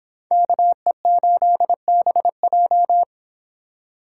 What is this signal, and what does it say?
Morse code, heard as a keyed tone, about 30 wpm, 700 Hz. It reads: KE8BJ